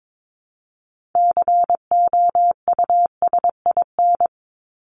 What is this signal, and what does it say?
Morse code, heard as a keyed tone, about 22 wpm, 700 Hz. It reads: COUSIN
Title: Top 1501-1750 Words 22wpm